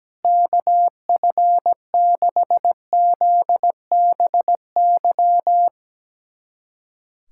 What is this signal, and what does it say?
Morse code, heard as a keyed tone, about 17 wpm, 700 Hz. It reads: KF6ZBY
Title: USA Callsign DE Callsign - 6 Characters - 17wpm